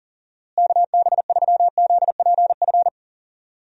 Morse code, heard as a keyed tone, about 38 wpm, 700 Hz. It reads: KB3ZPF
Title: USA Call Signs - 6 Characters - Repeated 3x - 40wpm